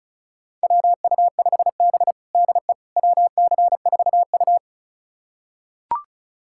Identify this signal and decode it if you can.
Morse code, heard as a keyed tone, about 36 wpm, 700 Hz. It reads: WU5B DE WC4U
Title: USA Callsign DE Callsign - 4 Characters - 35wpm